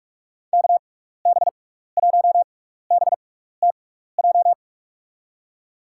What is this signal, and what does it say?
Morse code, heard as a keyed tone, about 46 wpm, spaced as for 16 wpm, 700 Hz. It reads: KB1BTJ
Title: USA Call Signs - 6 Characters - 6x Character Spacing - 45wpm